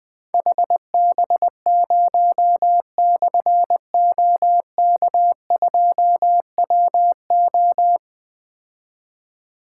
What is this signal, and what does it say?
Morse code, heard as a keyed tone, about 20 wpm, 700 Hz. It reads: HB0/OK2WO
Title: Difficult Call Signs 20wpm